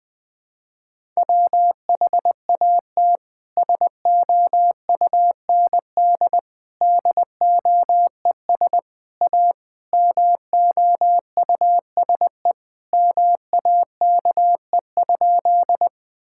Morse code, heard as a keyed tone, about 19 wpm, 700 Hz. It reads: WHAT SOUND DOES A MOUSE MAKE?
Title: Sentences from Top 1000 Words - Review Part 5 - 20wpm